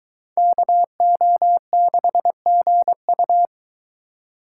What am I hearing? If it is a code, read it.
Morse code, heard as a keyed tone, about 22 wpm, 700 Hz. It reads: KO6GU